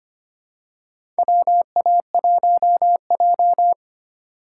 Morse code, heard as a keyed tone, about 25 wpm, 700 Hz. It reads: WA1J
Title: USA Call Signs - 4 Characters - 2x Character Spacing - 25wpm